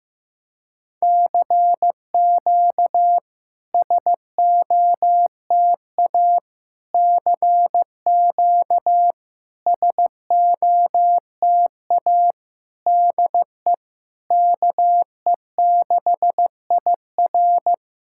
Morse code, heard as a keyed tone, about 15 wpm, 700 Hz. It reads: CQ SOTA CQ SOTA DE KE6IR
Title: SOTA QSO Practice 15wpm